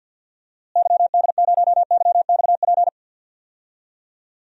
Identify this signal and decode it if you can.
Morse code, heard as a keyed tone, about 52 wpm, 700 Hz. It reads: YD0YXP
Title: DXCC-100 Call Signs - 6 Characters - Repeated 2x - 50wpm